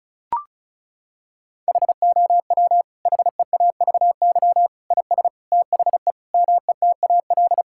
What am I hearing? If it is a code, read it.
Morse code, heard as a keyed tone, about 36 wpm, 700 Hz. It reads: HOW HEAVY IS THE METAL
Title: Sentences from Top 1000 Words - Review Part 6 - 35wpm